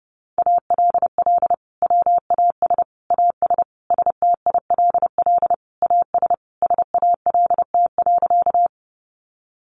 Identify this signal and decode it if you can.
Morse code, heard as a keyed tone, about 30 wpm, 700 Hz. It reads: ALL WAS AS STILL AS SALT.
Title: Sentences from Top 800 Words 30wpm